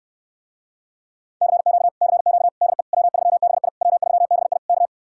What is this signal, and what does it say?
Morse code, heard as a keyed tone, about 69 wpm, 700 Hz. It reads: CQ CQ DE W3BI W3BI K